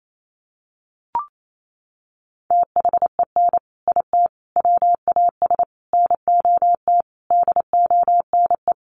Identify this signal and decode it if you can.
Morse code, heard as a keyed tone, about 28 wpm, 700 Hz. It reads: THEN IT WAS NOT DONE
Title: Sentences from Top 1300 Words 28wpm